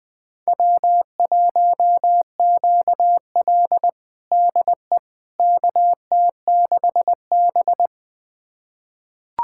Morse code, heard as a keyed tone, about 20 wpm, 700 Hz. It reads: W1QL DE KT6B
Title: USA Callsign DE Callsign - 4 Characters - 20wpm